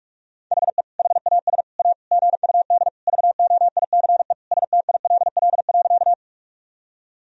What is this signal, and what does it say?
Morse code, heard as a keyed tone, about 46 wpm, 690 Hz. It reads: HE HAS A GUD VOICE STILL.
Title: Sentences from Top 500 Words and Abbreviations 45wpm